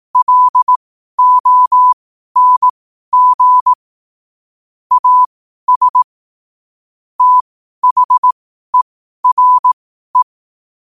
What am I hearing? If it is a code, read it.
Morse code, heard as a keyed tone, about 18 wpm, spaced as for 12 wpm, 1000 Hz. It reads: LONG AS THERE
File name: qotdfiles / 12wpm